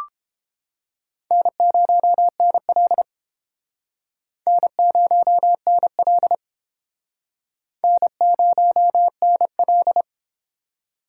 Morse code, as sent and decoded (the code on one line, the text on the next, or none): -. ----- -. .-.. / -. ----- -. .-.. / -. ----- -. .-..
N0NL N0NL N0NL